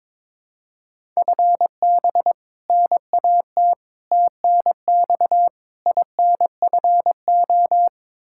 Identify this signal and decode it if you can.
Morse code, heard as a keyed tone, about 22 wpm, 700 Hz. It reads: FB NAT TNX INFO
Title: Practice QSO - Exchange 3 - More Info - 22wpm